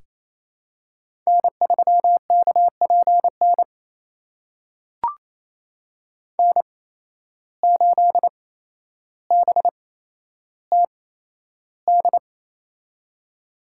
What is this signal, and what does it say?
Morse code, heard as a keyed tone, about 28 wpm, spaced as for 12 wpm, 700 Hz. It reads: N3KPN N8BTD
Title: USA Call Signs - 5 Characters - 8x Character Spacing - 28wpm